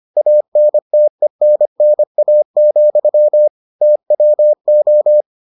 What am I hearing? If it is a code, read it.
Morse code, heard as a keyed tone, about 25 wpm, 600 Hz. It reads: ANTENNA, TWO